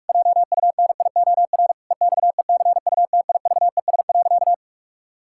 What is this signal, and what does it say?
Morse code, heard as a keyed tone, about 46 wpm, 700 Hz. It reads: JUNIOR EXECUTIVES.
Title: Pangram Challenge v2 - 45wpm